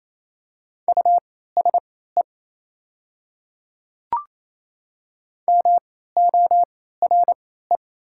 Morse code, heard as a keyed tone, about 28 wpm, spaced as for 16 wpm, 690 Hz. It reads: USE MORE
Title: 3x Character Spacing - Top 500 Words - 28wpm